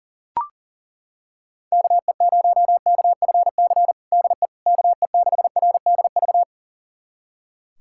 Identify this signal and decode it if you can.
Morse code, heard as a keyed tone, about 40 wpm, 700 Hz. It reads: KE0KFC DE KE6RDV
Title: USA Callsign DE Callsign - 6 Characters - 40wpm